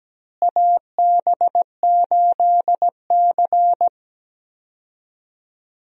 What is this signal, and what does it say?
Morse code, heard as a keyed tone, about 17 wpm, 700 Hz. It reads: AB8C
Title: USA Call Signs - 4 Characters - Repeated 2x - 17wpm